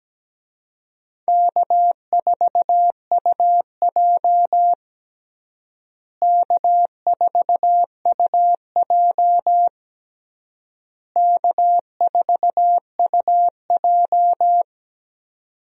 Morse code, heard as a keyed tone, about 17 wpm, 700 Hz. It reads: K4UJ K4UJ K4UJ
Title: USA Call Signs - 4 Characters - Repeated 3x - 17wpm